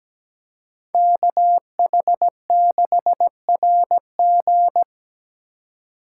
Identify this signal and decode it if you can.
Morse code, heard as a keyed tone, about 20 wpm, 700 Hz. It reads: KH6RG
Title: Speed Racing - USA Call Signs 17wpm